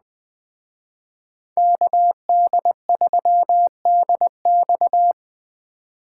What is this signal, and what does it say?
Morse code, heard as a keyed tone, about 20 wpm, 700 Hz. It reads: KD3DX